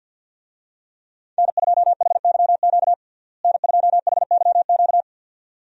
Morse code, heard as a keyed tone, about 50 wpm, 700 Hz. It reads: N2HYQ N2HYQ
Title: USA Call Signs - 5 Characters - Repeated 2x - 50wpm